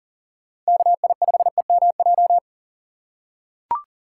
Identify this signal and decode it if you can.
Morse code, heard as a keyed tone, about 37 wpm, 700 Hz. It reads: KI5EMJ
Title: USA Call Signs - 3x Character Spacing - 40wpm